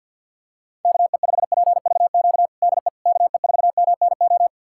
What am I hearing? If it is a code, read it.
Morse code, heard as a keyed tone, about 50 wpm, 700 Hz. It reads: KE5PVQ DE KE4MNO
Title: USA Callsign DE Callsign 50wpm